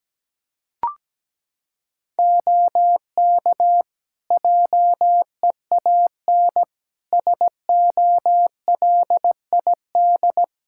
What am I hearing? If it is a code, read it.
Morse code, heard as a keyed tone, about 17 wpm, 700 Hz. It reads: OK JEAN SOLID
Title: Practice QSO - Exchange 3 - More Info - 17wpm